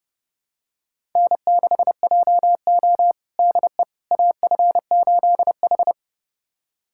Morse code, heard as a keyed tone, about 30 wpm, 700 Hz. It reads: N6JO DE AF8H
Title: USA Callsign DE Callsign - 4 Characters - 30wpm